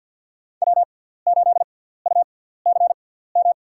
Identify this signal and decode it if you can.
Morse code, heard as a keyed tone, about 53 wpm, spaced as for 17 wpm, 700 Hz. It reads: W8UCM
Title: USA Call Signs - 5 Characters - 6x Character Spacing - 50wpm